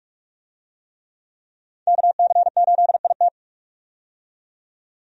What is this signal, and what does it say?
Morse code, heard as a keyed tone, about 43 wpm, 700 Hz. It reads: KC8IT
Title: USA Call Signs - 5 Characters - 45wpm